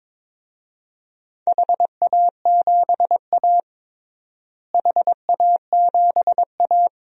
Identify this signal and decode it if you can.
Morse code, heard as a keyed tone, about 22 wpm, 700 Hz. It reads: HA7A HA7A
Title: DXCC-100 Call Signs - 4 Characters - Repeated 3x - 22wpm